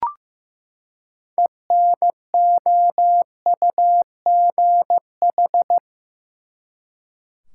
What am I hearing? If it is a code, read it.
Morse code, heard as a keyed tone, about 15 wpm, 700 Hz. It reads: ENOUGH
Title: Six Letter Words - 15wpm